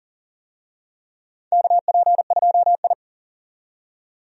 Morse code, heard as a keyed tone, about 40 wpm, 700 Hz. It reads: KP2I
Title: DXCC-100 Call Signs - 4 Characters - 40wpm